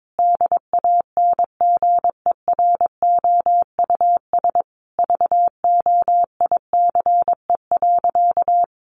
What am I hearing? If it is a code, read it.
Morse code, heard as a keyed tone, about 22 wpm, 700 Hz. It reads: DANGEROUS VOICE.
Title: Sentences from Top 1500 Words 22wpm